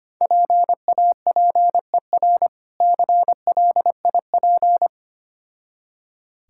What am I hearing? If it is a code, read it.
Morse code, heard as a keyed tone, about 25 wpm, 700 Hz. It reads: PAPER CLIP
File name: practice / 25wpm / Difficult Words with Hints - 25wpm